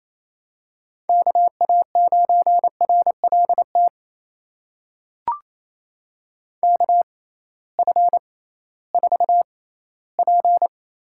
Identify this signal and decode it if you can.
Morse code, heard as a keyed tone, about 29 wpm, spaced as for 17 wpm, 700 Hz. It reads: KA9RLT KF4P
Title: USA Call Signs - 6x Character Spacing - 28wpm